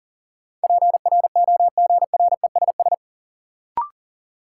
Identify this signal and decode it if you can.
Morse code, heard as a keyed tone, about 40 wpm, 700 Hz. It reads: PROGRESS
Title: Sentences from Top 1400 Words - New Word Drill - 40wpm